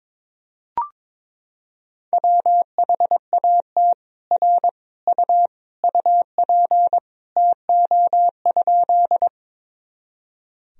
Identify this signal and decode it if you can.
Morse code, heard as a keyed tone, about 22 wpm, 700 Hz. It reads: WHAT R U UP TO?
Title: Sentences from Top 100 Words and Abbreviations 22wpm